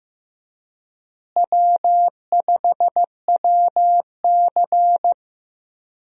Morse code, heard as a keyed tone, about 16 wpm, 700 Hz. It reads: W5WC
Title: USA Call Signs - 4 Characters - Repeated 3x - 15wpm